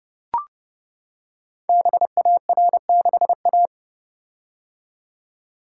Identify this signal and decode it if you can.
Morse code, heard as a keyed tone, about 30 wpm, 700 Hz. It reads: BAR6A
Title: Code Groups - 5 Letter:Numbers - 30wpm